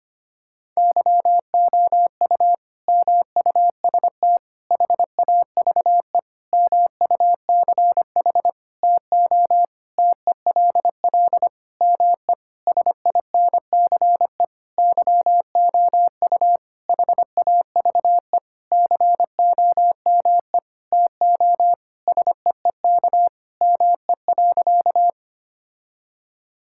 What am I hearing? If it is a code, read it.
Morse code, heard as a keyed tone, about 25 wpm, 700 Hz. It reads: YOU MUST HAVE MUCH TO TELL ME SINCE YOU HAVE COME TO SEEK ME.
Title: Sentences from Top 1600 Words 25wpm